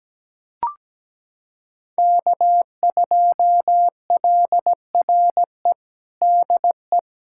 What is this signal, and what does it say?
Morse code, heard as a keyed tone, about 17 wpm, 700 Hz. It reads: K2LRE DE
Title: USA Callsign DE Callsign - 5 Characters - 17wpm